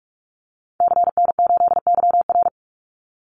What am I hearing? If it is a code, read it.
Morse code, heard as a keyed tone, about 44 wpm, 700 Hz. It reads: CN8YR
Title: Call Signs Repeated 2x - 45wpm